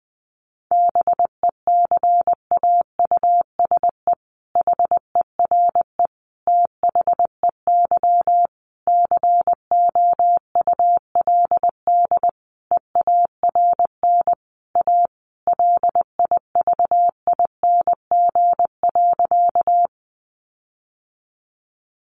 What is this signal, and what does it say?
Morse code, heard as a keyed tone, about 20 wpm, 700 Hz. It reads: BECAUSE HERE THEY COULD EARN A LIVING.
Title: Sentences from Top 1500 Words 20wpm